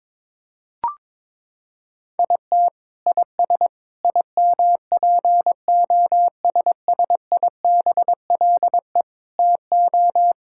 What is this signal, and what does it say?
Morse code, heard as a keyed tone, about 22 wpm, 700 Hz. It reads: IT IS IMPOSSIBLE TO
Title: Sentences from Top 1600 Words 22wpm